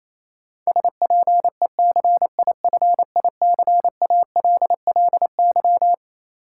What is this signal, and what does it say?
Morse code, heard as a keyed tone, about 28 wpm, 700 Hz. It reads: SPECIFICALLY
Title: Sentences from Top 1100 Words - New Word Drill - 28wpm